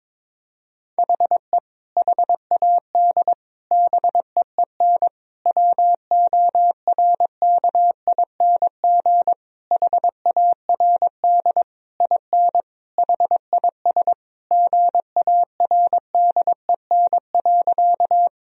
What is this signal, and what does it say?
Morse code, heard as a keyed tone, about 22 wpm, 700 Hz. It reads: HE HAD BEEN WORKING HARD IN HIS GARDEN.
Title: Sentences from Top 1000 Words - Review Part 4 - 22wpm